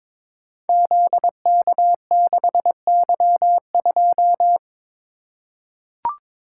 Morse code, heard as a keyed tone, about 22 wpm, 700 Hz. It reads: ZK6Y2
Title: Code Groups - 5 Letter:Numbers - 22wpm